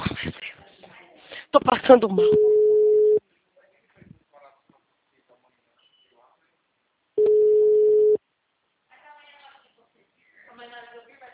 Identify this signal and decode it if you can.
Ringback tone (Europe, 425 Hz), with speech.
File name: out- -2019- - - .257